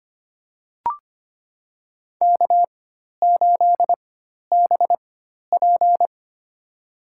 Morse code, heard as a keyed tone, about 25 wpm, spaced as for 12 wpm, 700 Hz. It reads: K8BP
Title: USA Call Signs - 4 Characters - 4x Character Spacing - 25wpm